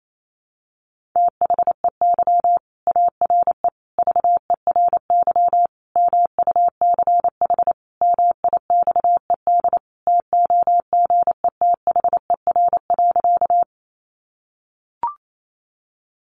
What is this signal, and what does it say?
Morse code, heard as a keyed tone, about 28 wpm, 700 Hz. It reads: THEY ARE VERY MUCH MIXED TOGETHER.
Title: Sentences from Top 1500 Words 28wpm